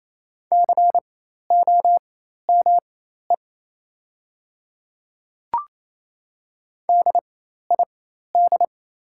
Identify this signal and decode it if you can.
Morse code, heard as a keyed tone, about 28 wpm, spaced as for 13 wpm, 700 Hz. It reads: COME DID
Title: 4x Character Spacing - Top 100 Words - 28wpm